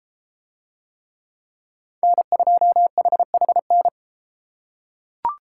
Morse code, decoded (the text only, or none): N2HHN